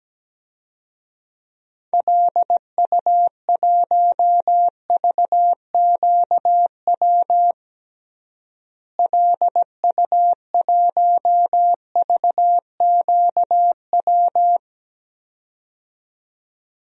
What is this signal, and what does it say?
Morse code, heard as a keyed tone, about 17 wpm, 700 Hz. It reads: LU1VQW LU1VQW
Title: DXCC-100 Call Signs - 6 Characters - Repeated 2x - 17wpm